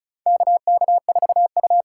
Morse code, heard as a keyed tone, about 36 wpm, 700 Hz. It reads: KK4U
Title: USA Call Signs - 6x Character Spacing - 35wpm